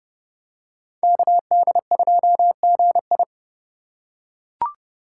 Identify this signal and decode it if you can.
Morse code, heard as a keyed tone, about 27 wpm, 700 Hz. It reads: KD2GI